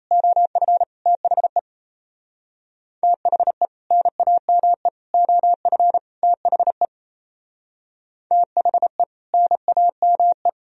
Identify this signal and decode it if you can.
Morse code, heard as a keyed tone, about 36 wpm, 700 Hz. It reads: OF THE THE NAME OF THE THE NAME